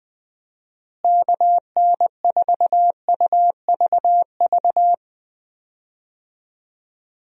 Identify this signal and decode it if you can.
Morse code, heard as a keyed tone, about 20 wpm, 700 Hz. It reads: KN4UVV
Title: USA Call Signs - 6 Characters - Repeated 2x - 20wpm